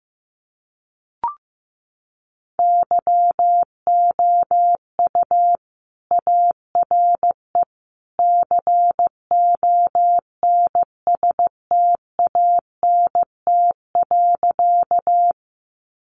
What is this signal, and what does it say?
Morse code, heard as a keyed tone, about 15 wpm, 700 Hz. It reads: YOU ARE CONSTANT.
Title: Sentences from Top 1500 Words 15wpm